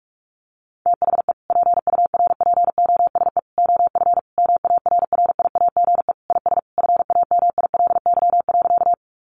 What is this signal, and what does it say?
Morse code, heard as a keyed tone, about 44 wpm, 700 Hz. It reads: THE PURPOSE OF MARRIAGE IS FAMILY.
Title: Sentences from Top 800 Words 45wpm